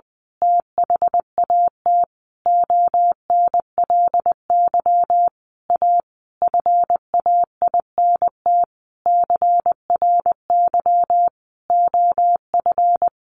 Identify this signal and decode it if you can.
Morse code, heard as a keyed tone, about 20 wpm, 700 Hz. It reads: THAT ONLY A FAINT CRY OF
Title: A Sound Of Thunder 20wpm